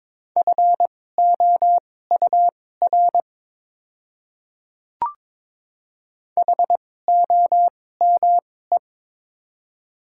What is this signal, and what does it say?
Morse code, heard as a keyed tone, about 22 wpm, spaced as for 16 wpm, 700 Hz. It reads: FOUR HOME
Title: 2x Character Spacing - Top 300 Words - 22wpm